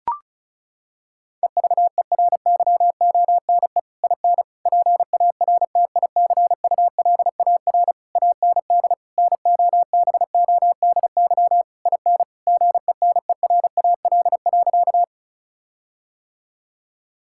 Morse code, heard as a keyed tone, about 36 wpm, 700 Hz. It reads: EVERYONE IN PARTICULAR AND NOBODY IN GENERAL.
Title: Sentences from Top 1600 Words 35wpm